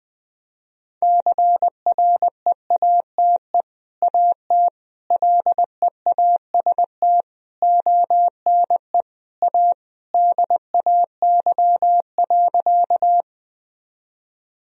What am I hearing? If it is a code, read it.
Morse code, heard as a keyed tone, about 20 wpm, 700 Hz. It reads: CREATE AT LEAST ONE A DAY.
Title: Sentences from Top 1000 Words - Review Part 3 - 20wpm